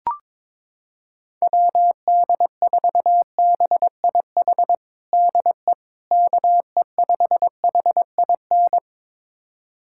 Morse code, heard as a keyed tone, about 22 wpm, 700 Hz. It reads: WD4BIH DE KE5HIN
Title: USA Callsign DE Callsign 22wpm